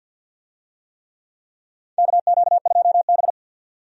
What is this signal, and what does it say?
Morse code, heard as a keyed tone, about 53 wpm, 700 Hz. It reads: KQ2B
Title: USA Call Signs - 4 Characters - 50wpm